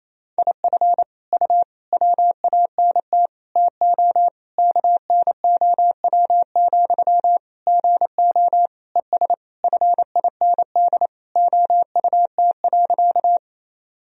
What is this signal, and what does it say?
Morse code, heard as a keyed tone, about 28 wpm, 700 Hz. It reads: IF U WANT TO KNOW, GO ES FIND OUT.
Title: Sentences from Top 200 Words and Abbreviations 28wpm